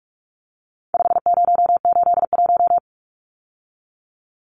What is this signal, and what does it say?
Morse code, heard as a keyed tone, about 44 wpm, 700 Hz. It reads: 5081
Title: Numbers - 4-Digit 45wpm